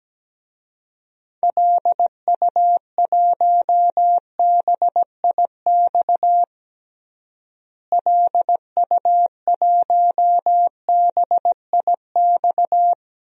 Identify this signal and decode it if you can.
Morse code, heard as a keyed tone, about 17 wpm, 700 Hz. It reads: LU1BIX LU1BIX
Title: DXCC-100 Call Signs - 6 Characters - Repeated 3x - 17wpm